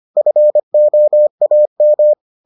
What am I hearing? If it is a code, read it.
Morse code, heard as a keyed tone, about 25 wpm, 600 Hz. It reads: FOAM